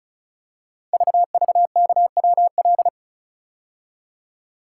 Morse code, heard as a keyed tone, about 36 wpm, 700 Hz. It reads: VVKWL
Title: Code Groups - 5 Letters - 35wpm